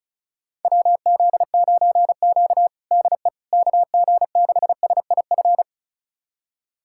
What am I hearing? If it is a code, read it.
Morse code, heard as a keyed tone, about 36 wpm, 700 Hz. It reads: WZ9Q DE KG6SIF
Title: USA Callsign DE Callsign 35wpm